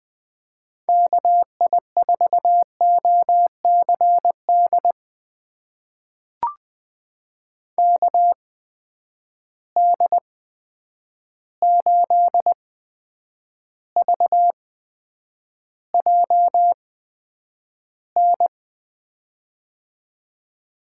Morse code, heard as a keyed tone, about 21 wpm, spaced as for 9 wpm, 700 Hz. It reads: KI4OCD KD8VJN
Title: USA Call Signs - 8x Character Spacing - 20wpm